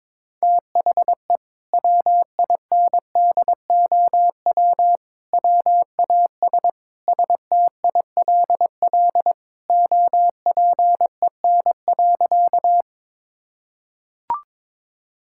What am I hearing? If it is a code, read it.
Morse code, heard as a keyed tone, about 22 wpm, 700 Hz. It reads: THE WINDOW WAS STILL OPEN.
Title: Sentences from Top 1600 Words 22wpm